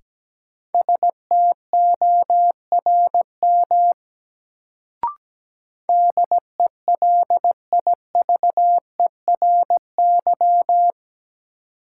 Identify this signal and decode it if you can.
Morse code, heard as a keyed tone, about 17 wpm, 700 Hz. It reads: STORM DELIVERY
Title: Sentences from Top 1500 Words - New Word Drill - 17wpm